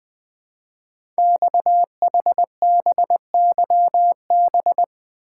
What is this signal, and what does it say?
Morse code, heard as a keyed tone, about 20 wpm, 700 Hz. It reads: XHBYB
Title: Code Groups - 5 Letters - 20wpm